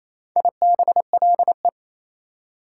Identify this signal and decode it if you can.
Morse code, heard as a keyed tone, about 28 wpm, 700 Hz. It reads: IBLE
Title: Prefixes and Suffixes 28wpm